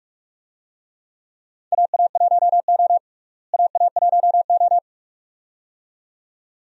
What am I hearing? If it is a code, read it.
Morse code, heard as a keyed tone, about 46 wpm, 700 Hz. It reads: AA1O AA1O
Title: USA Call Signs - 4 Characters - Repeated 2x - 45wpm